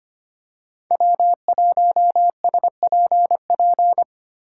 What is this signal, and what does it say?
Morse code, heard as a keyed tone, about 25 wpm, 700 Hz. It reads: W1SPP
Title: USA Call Signs - 5 Characters - 3x Character Spacing - 25wpm